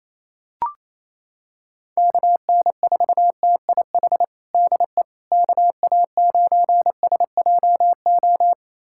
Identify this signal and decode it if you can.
Morse code, heard as a keyed tone, about 27 wpm, 700 Hz. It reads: KN4TIH DE KA9SJO
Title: USA Callsign DE Callsign - 6 Characters - 28wpm